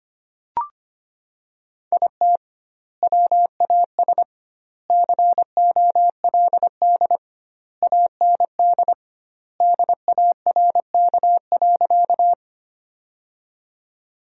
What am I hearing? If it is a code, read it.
Morse code, heard as a keyed tone, about 25 wpm, 700 Hz. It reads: IT WAS COLD AND DARK.
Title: 2x Word-Spacing - Sentences from Top 500 Words - 25wpm